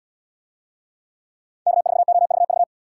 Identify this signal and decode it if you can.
Morse code, heard as a keyed tone, about 77 wpm, 700 Hz. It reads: K4OVV